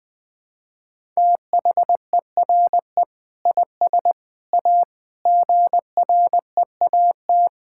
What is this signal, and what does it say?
Morse code, heard as a keyed tone, about 21 wpm, 700 Hz. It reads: THERE IS A GREAT
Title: Sets of 6 Words - 20wpm